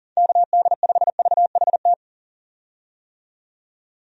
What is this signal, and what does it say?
Morse code, heard as a keyed tone, about 37 wpm, 700 Hz. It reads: KD5VHT
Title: USA Call Signs - 6 Characters - 40wpm